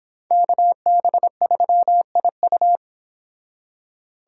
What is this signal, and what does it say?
Morse code, heard as a keyed tone, about 30 wpm, 700 Hz. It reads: KB3IU